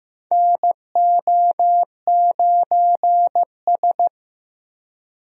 Morse code, heard as a keyed tone, about 15 wpm, 700 Hz. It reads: NO9S
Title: USA Call Signs - 4 Characters - Repeated 3x - 15wpm